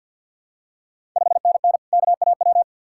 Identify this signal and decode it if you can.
Morse code, heard as a keyed tone, about 50 wpm, 700 Hz. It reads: 5NN KAW
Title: Ontario QSO Party Exchange 50wpm